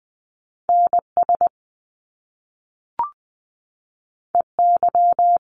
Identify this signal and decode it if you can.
Morse code, heard as a keyed tone, about 20 wpm, 700 Hz. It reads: NS EY
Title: Top Two Letter Combinations in Words 20wpm